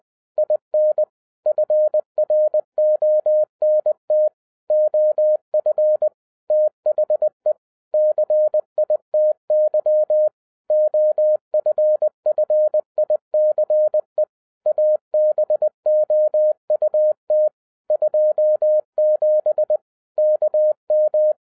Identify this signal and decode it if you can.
Morse code, heard as a keyed tone, about 20 wpm, 610 Hz. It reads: IN FRONT OF THE CITY OFFICE ABOUT 27 KM
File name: 2011-10-mp3 / n03-TARO JR0QWW 20wpm